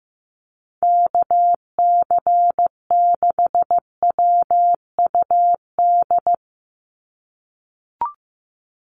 Morse code, heard as a keyed tone, about 15 wpm, 700 Hz. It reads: KC6WUD
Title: USA Call Signs 15wpm